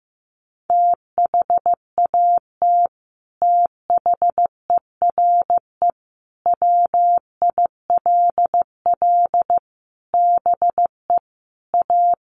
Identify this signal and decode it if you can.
Morse code, heard as a keyed tone, about 15 wpm, 700 Hz. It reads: THAT THERE WILL BE A
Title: Sets of 5 Words Encore2 15wpm